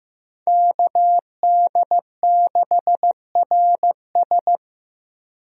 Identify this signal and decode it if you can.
Morse code, heard as a keyed tone, about 15 wpm, 700 Hz. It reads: KD6RS
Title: USA Call Signs - 5 Characters - Repeated 3x - 15wpm